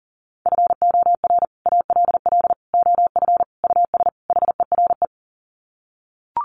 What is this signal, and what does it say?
Morse code, heard as a keyed tone, about 40 wpm, 700 Hz. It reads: FOR ALL OF US HERE
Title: Sets of 5 Words 40wpm